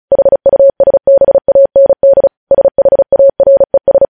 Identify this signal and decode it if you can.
Morse code, heard as a keyed tone, about 36 wpm, 550 Hz. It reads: HUSBAND SHARES
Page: News headlines in Morse code at 35 wpm